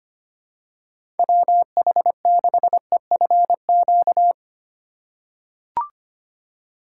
Morse code, heard as a keyed tone, about 25 wpm, 700 Hz. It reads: WH6EFQ
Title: USA Call Signs - 6 Characters - 3x Character Spacing - 25wpm